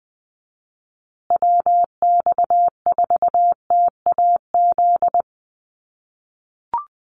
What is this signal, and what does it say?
Morse code, heard as a keyed tone, about 20 wpm, 700 Hz. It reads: WX4TAZ
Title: USA Call Signs Repeated 2x 20wpm